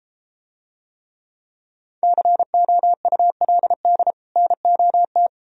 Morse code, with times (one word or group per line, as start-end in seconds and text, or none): 2.0-4.1 COULD
4.4-5.3 NOT